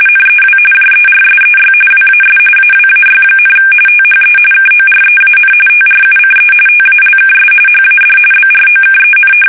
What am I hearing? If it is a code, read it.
RTTY, about 45.45 baud. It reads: NEW JERSEY WAS AWAY, THE KOREAN TRUCE TALKS HAD BROKEN
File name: fsk850 / battleshipnj